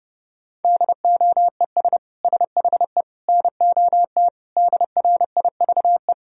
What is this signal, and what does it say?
Morse code, heard as a keyed tone, about 30 wpm, 700 Hz. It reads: DOES SHE NOT DRIVE
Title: Sentences from Top 1000 Words - Review Part 2 - 30wpm